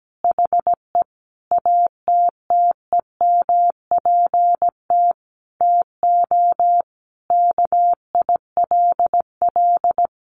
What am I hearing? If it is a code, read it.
Morse code, heard as a keyed tone, about 17 wpm, 700 Hz. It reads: HE ATTEMPT TO KILL
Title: Sentences from Top 1400 Words 17wpm